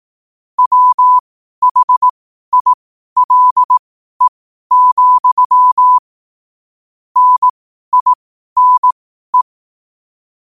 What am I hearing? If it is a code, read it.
Morse code, heard as a keyed tone, about 18 wpm, spaced as for 12 wpm, 1000 Hz. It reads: WHILE, NINE
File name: qotdfiles / 12wpm